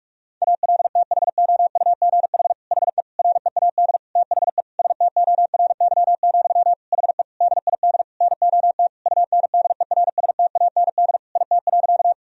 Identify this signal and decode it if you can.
Morse code, heard as a keyed tone, about 46 wpm, 700 Hz. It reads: ALTHOUGH HE READ THE STORY, HE DID NOT UNDERSTAND IT.
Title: Sentences from Top 1000 Words - Review Part 3 - 45wpm